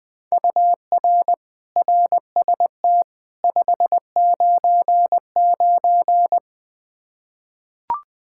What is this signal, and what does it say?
Morse code, heard as a keyed tone, about 21 wpm, 700 Hz. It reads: UR RST 599
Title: K1USN SST QSO Practice 20wpm